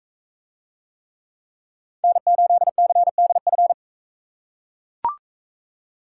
Morse code, heard as a keyed tone, about 40 wpm, 700 Hz. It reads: N8CDF